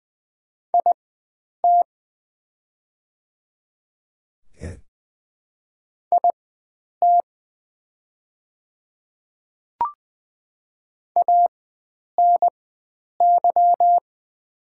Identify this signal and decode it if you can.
Morse code, heard as a keyed tone, about 20 wpm, spaced as for 9 wpm, 700 Hz. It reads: IT IT ANY
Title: 4x Character Spacing - Top 300 Words - 20wpm